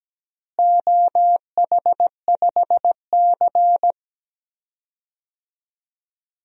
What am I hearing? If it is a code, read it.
Morse code, heard as a keyed tone, about 17 wpm, 700 Hz. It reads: OH5C
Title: DXCC-100 Call Signs - 4 Characters - Repeated 3x - 17wpm